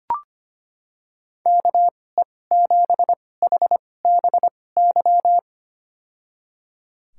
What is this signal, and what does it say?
Morse code, heard as a keyed tone, about 25 wpm, spaced as for 18 wpm, 700 Hz. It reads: KE7HBY
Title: USA Call Signs - 2x Character Spacing - 25wpm